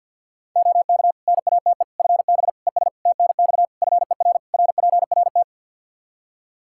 Morse code, heard as a keyed tone, about 47 wpm, 700 Hz. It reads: OK NATE FB ES TNX FER RPRT
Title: Practice QSO - Exchange 2 - Info - 50wpm